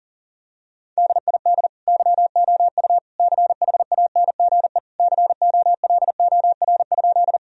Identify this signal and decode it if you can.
Morse code, heard as a keyed tone, about 40 wpm, 700 Hz. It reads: DID YOU CHANGE COLOR?
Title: Sentences from Top 1000 Words - Review Part 2 - 40wpm